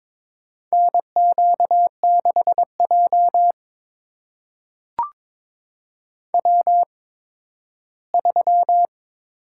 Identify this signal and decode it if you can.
Morse code, heard as a keyed tone, about 22 wpm, spaced as for 13 wpm, 700 Hz. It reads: NQ6J W3
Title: USA Call Signs - 4 Characters - 8x Character Spacing - 22wpm